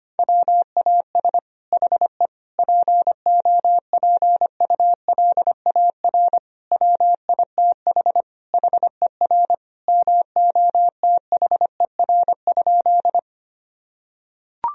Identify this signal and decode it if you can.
Morse code, heard as a keyed tone, about 25 wpm, 700 Hz. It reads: WAS HE POPULAR WITH HER MOTHER?
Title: Sentences from Top 1000 Words - Review Part 3 - 25wpm